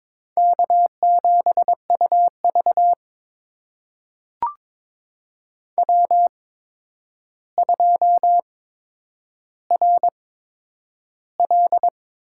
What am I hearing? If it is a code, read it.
Morse code, heard as a keyed tone, about 22 wpm, spaced as for 9 wpm, 700 Hz. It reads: K7UV W2RL
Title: USA Call Signs - 4 Characters - 8x Character Spacing - 22wpm